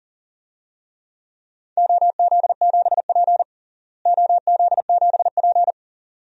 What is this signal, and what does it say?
Morse code, heard as a keyed tone, about 40 wpm, 700 Hz. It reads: OZ7P OZ7P
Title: DXCC-100 Call Signs - 4 Characters - Repeated 2x - 40wpm